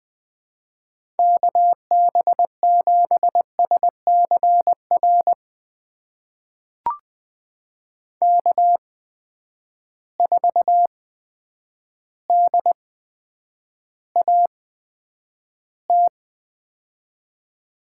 Morse code, heard as a keyed tone, about 21 wpm, spaced as for 9 wpm, 700 Hz. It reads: KB7SCR K4DAT
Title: USA Call Signs - 8x Character Spacing - 20wpm